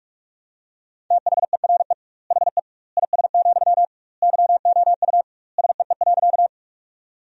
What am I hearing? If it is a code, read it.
Morse code, heard as a keyed tone, about 46 wpm, 700 Hz. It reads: THERE HE IS, YOU SEE.
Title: 2x Word-Spacing - Sentences from Top 100 Words - 45wpm